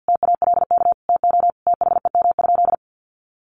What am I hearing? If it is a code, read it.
Morse code, heard as a keyed tone, about 48 wpm, 700 Hz. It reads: TALK TO THEM?
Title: Sentences from Top 1400 Words 50wpm